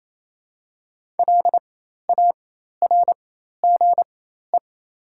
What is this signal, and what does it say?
Morse code, heard as a keyed tone, about 27 wpm, spaced as for 13 wpm, 700 Hz. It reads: LARGE
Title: 4x Character Spacing - Top 200 Words - 28wpm